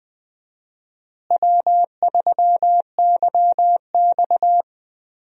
Morse code, heard as a keyed tone, about 20 wpm, 700 Hz. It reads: W3YX
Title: USA Call Signs - 4 Characters - Repeated 3x - 20wpm